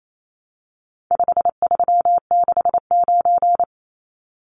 Morse code, heard as a keyed tone, about 28 wpm, 700 Hz. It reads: 5369
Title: Numbers - 4-Digit 28wpm